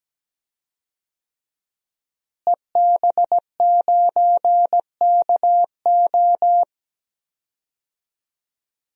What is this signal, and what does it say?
Morse code, heard as a keyed tone, about 17 wpm, 700 Hz. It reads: EB9KO
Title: DXCC-100 Call Signs - 5 Characters - Repeated 3x - 17wpm